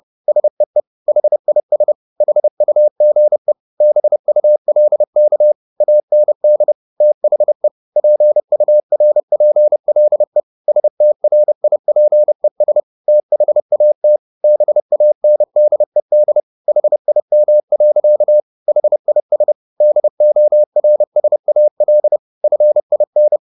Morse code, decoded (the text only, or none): SEE HIS HUGE BULK AND THE PURPLE STRIPES THAT BANDED HIM. HIS DORSAL FIN